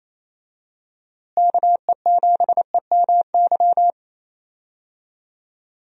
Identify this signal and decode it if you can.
Morse code, heard as a keyed tone, about 27 wpm, 700 Hz. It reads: KE7EMY
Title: USA Call Signs - 6 Characters - Repeated 3x - 28wpm